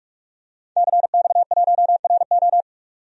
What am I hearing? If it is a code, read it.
Morse code, heard as a keyed tone, about 44 wpm, 690 Hz. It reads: CX1RO
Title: DXCC-100 Call Signs - 5 Characters - 45wpm